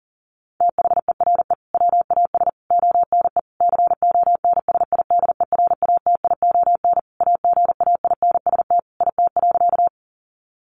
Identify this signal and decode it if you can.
Morse code, heard as a keyed tone, about 37 wpm, 700 Hz. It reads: THERE WAS ONE CONSIDERATION AGAINST IT.
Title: Sentences from Top 1500 Words 40wpm